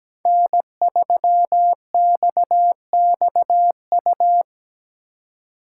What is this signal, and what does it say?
Morse code, heard as a keyed tone, about 20 wpm, 700 Hz. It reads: N3XXU
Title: Speed Racing - USA 1x3 Call Signs 17wpm